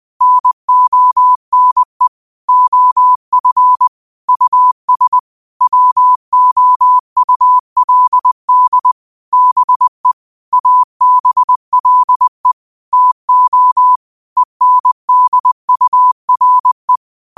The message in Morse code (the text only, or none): NONE OF US WOULD BE ABLE TO ENDURE